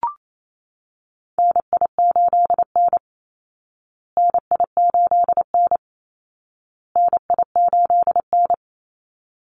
Morse code, heard as a keyed tone, about 28 wpm, 700 Hz. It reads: NI8N NI8N NI8N
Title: Call Signs Repeated 2x - 28wpm